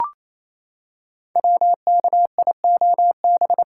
Morse code, heard as a keyed tone, about 28 wpm, 700 Hz. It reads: WKIOB
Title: Code Groups - 5 Letters - 28wpm